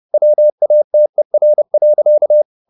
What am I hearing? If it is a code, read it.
Morse code, heard as a keyed tone, about 30 wpm, 600 Hz. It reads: WATER.